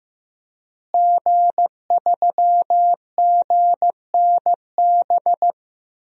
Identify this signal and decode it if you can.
Morse code, heard as a keyed tone, about 15 wpm, 700 Hz. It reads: G3GNB
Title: DXCC-100 Call Signs - 5 Characters - Repeated 3x - 15wpm